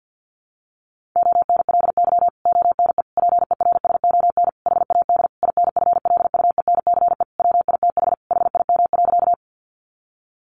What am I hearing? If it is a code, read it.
Morse code, heard as a keyed tone, about 50 wpm, 700 Hz. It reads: ONLY ONE PERSON HAD INFLUENCE WITH HIM.
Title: Sentences from Top 1100 Words 50wpm